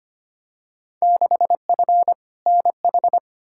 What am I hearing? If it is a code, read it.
Morse code, heard as a keyed tone, about 25 wpm, 700 Hz. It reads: 6F NH
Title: Field Day Exchanges Encore - 25wpm